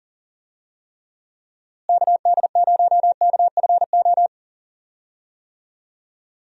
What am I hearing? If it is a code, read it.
Morse code, heard as a keyed tone, about 40 wpm, 700 Hz. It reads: KD0KFO
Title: USA Call Signs - 6 Characters - Repeated 3x - 40wpm